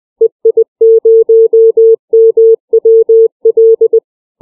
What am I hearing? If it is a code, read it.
Morse code, heard as a keyed tone, about 20 wpm, 450 Hz. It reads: EI0MWL
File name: CW / callsign